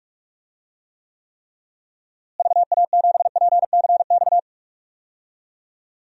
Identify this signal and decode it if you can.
Morse code, heard as a keyed tone, about 47 wpm, 700 Hz. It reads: VA7PCX
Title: DXCC-100 Call Signs - 6 Characters - Repeated 3x - 45wpm